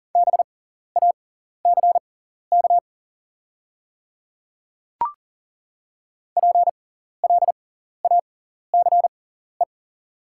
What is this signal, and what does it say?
Morse code, heard as a keyed tone, about 40 wpm, spaced as for 14 wpm, 690 Hz. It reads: BACK PLACE
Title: 6x Character Spacing - Top 200 Words - 40wpm